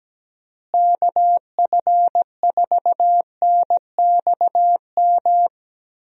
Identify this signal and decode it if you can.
Morse code, heard as a keyed tone, about 17 wpm, 700 Hz. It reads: KF4NXM
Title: USA Call Signs - 6 Characters - Repeated 2x - 17wpm